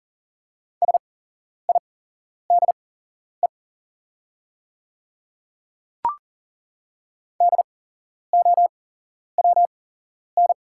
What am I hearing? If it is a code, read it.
Morse code, heard as a keyed tone, about 40 wpm, spaced as for 11 wpm, 700 Hz. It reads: SIDE DOWN
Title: 8x Character Spacing - Top 100 Words - 40wpm